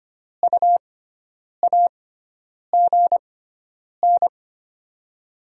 Morse code, heard as a keyed tone, about 25 wpm, spaced as for 9 wpm, 700 Hz. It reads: U AGN
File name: Top 20 QSO Elements - 6x Character Spacing - 25wpm